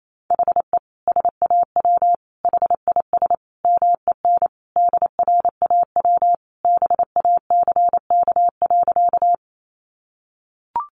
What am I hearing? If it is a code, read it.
Morse code, heard as a keyed tone, about 28 wpm, 700 Hz. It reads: HE SAW HIS MEN DRAW BACK.
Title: Sentences from Top 300 Words 28wpm